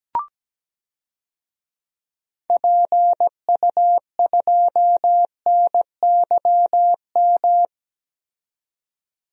Morse code, heard as a keyed tone, about 17 wpm, 700 Hz. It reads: PU2NYM
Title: DXCC-100 Call Signs - 6 Characters - 17wpm